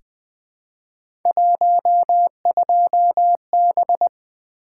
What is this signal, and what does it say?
Morse code, heard as a keyed tone, about 20 wpm, 700 Hz. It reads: 12B
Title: Three Letter-Number Combinations - 20wpm